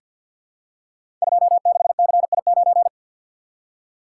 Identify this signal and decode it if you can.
Morse code, heard as a keyed tone, about 51 wpm, 700 Hz. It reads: 26CI9
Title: Code Groups - 5 Letter:Numbers - 50wpm